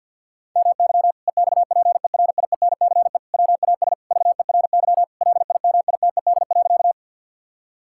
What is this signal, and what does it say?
Morse code, heard as a keyed tone, about 50 wpm, 700 Hz. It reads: MY EXPERIENCE WAS VERY LIMITED.
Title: Sentences from Top 1000 Words - Review Part 6 - 50wpm